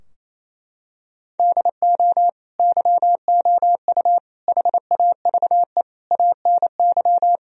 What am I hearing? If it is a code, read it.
Morse code, heard as a keyed tone, about 28 wpm, 700 Hz. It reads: DO YOU HAVE ANY
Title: Sentences from Top 1000 Words - Review Part 3 - 28wpm